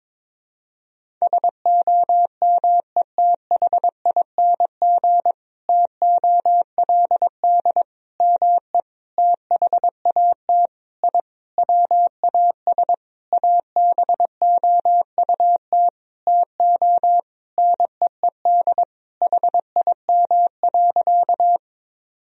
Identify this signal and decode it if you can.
Morse code, heard as a keyed tone, about 22 wpm, 700 Hz. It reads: SOMETHING TOLD ME THAT I WAS ABOUT TO NEED HIM.
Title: Sentences from Top 1000 Words - Review Part 1 - 22wpm